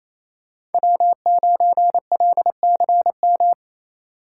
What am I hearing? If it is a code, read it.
Morse code, heard as a keyed tone, about 27 wpm, 700 Hz. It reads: W9LCM
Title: USA Call Signs - 5 Characters - 2x Character Spacing - 28wpm